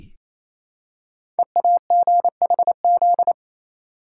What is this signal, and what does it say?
Morse code, heard as a keyed tone, about 27 wpm, 700 Hz. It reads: EAGHZ
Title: Code Groups - 5 Letters - 28wpm